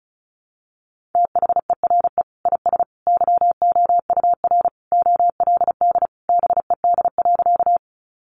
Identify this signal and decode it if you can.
Morse code, heard as a keyed tone, about 34 wpm, 700 Hz. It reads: THERE IS YOUR OLD BED.
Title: Sentences from Top 500 Words 35wpm